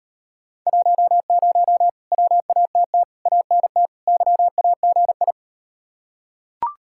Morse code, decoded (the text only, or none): 10 WATT ANT YAGI